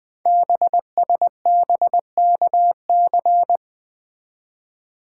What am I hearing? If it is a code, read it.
Morse code, heard as a keyed tone, about 20 wpm, 700 Hz. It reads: BSBKC
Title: Code Groups - 5 Letters - 20wpm